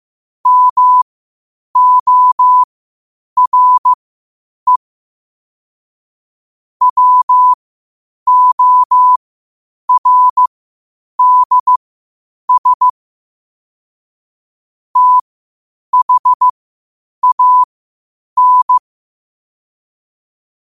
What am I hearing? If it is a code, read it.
Morse code, heard as a keyed tone, about 15 wpm, spaced as for 8 wpm, 1000 Hz. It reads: MORE WORDS THAN
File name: qotdfiles / 8wpm